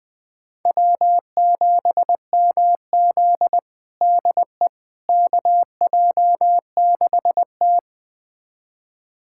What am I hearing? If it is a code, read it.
Morse code, heard as a keyed tone, about 20 wpm, 700 Hz. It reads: W7MZ DE KJ6T
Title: USA Callsign DE Callsign - 4 Characters - 20wpm